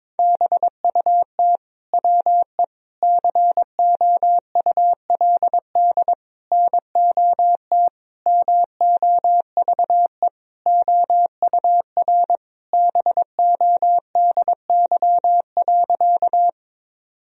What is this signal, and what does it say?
Morse code, heard as a keyed tone, about 22 wpm, 700 Hz. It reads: BUT WE COULD NOT MOVE OUR BODY.
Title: Sentences from Top 1000 Words - Review Part 2 - 22wpm